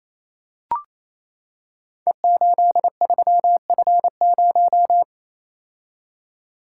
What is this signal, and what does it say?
Morse code, heard as a keyed tone, about 29 wpm, 700 Hz. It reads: E83F0
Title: Code Groups - 5 Letter:Numbers - 28wpm